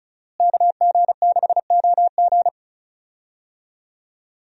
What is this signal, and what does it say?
Morse code, heard as a keyed tone, about 34 wpm, 700 Hz. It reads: KG6OG
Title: USA Call Signs - 5 Characters - Repeated 2x - 35wpm